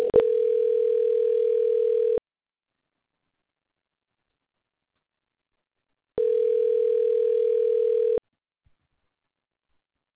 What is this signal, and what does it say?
Ringback tone (North America), 440 + 480 Hz.
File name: recordings / REc1592a3208c2ee203bdcabf40d24dd8c